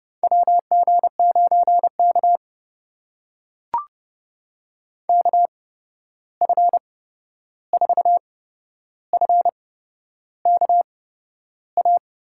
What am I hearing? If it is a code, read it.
Morse code, heard as a keyed tone, about 30 wpm, spaced as for 11 wpm, 700 Hz. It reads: WG9K KF4FKA
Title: USA Call Signs - 8x Character Spacing - 30wpm